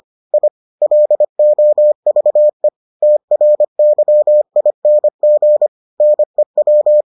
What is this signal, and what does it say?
Morse code, heard as a keyed tone, about 25 wpm, 600 Hz. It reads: I LOVE TRYING NEW